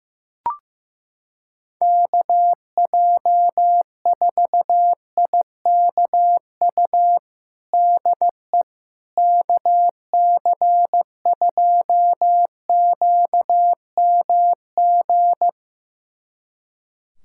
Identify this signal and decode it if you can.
Morse code, heard as a keyed tone, about 16 wpm, 700 Hz. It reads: KJ4IKU DE KC2QMG
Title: USA Callsign DE Callsign - 6 Characters - 15wpm